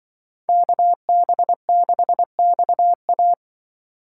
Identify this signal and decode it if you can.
Morse code, heard as a keyed tone, about 25 wpm, 700 Hz. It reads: KB6XA